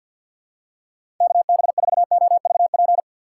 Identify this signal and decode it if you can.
Morse code, heard as a keyed tone, about 50 wpm, 700 Hz. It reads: KB4OVP
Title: USA Call Signs - 2x Character Spacing - 50wpm